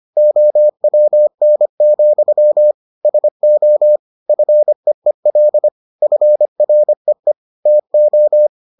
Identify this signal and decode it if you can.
Morse code, heard as a keyed tone, about 25 wpm, 600 Hz. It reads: OWN, SO FEEL FREE TO